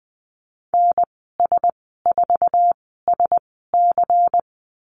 Morse code, heard as a keyed tone, about 20 wpm, spaced as for 15 wpm, 700 Hz. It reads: NS4SC
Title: 147 Lesson 24g - Letter y - Call Signs 20wpm